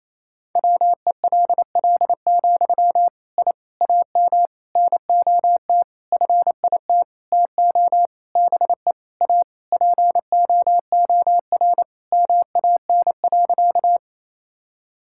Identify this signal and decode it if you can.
Morse code, heard as a keyed tone, about 28 wpm, 700 Hz. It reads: WELL, I AM NOT FIT TO BE A POOR MAN.
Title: Sentences from Top 1000 Words - Review Part 5 - 28wpm